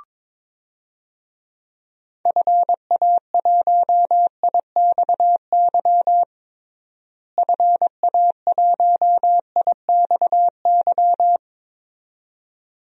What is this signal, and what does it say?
Morse code, heard as a keyed tone, about 22 wpm, 700 Hz. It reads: FA1IXY FA1IXY
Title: DXCC-100 Call Signs - 6 Characters - Repeated 2x - 22wpm